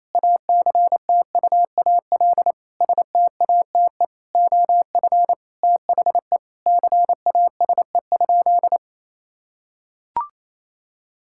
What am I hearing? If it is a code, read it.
Morse code, heard as a keyed tone, about 28 wpm, 700 Hz. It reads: ACTUAL STATE OF THE CASE?
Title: Sentences from Top 1000 Words - Review Part 5 - 28wpm